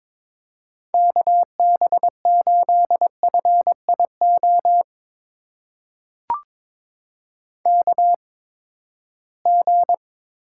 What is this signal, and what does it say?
Morse code, heard as a keyed tone, about 22 wpm, spaced as for 15 wpm, 700 Hz. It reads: KB8FIO KG
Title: USA Call Signs - 6 Characters - 8x Character Spacing - 22wpm